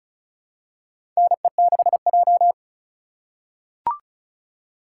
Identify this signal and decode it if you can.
Morse code, heard as a keyed tone, about 36 wpm, 700 Hz. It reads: NE6J
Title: USA Call Signs - 4 Characters - 4x Character Spacing - 35wpm